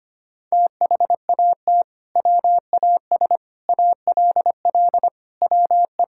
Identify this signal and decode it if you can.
Morse code, heard as a keyed tone, about 25 wpm, 700 Hz. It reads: THAT WAS ALL WE
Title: Sentences from Top 1600 Words 25wpm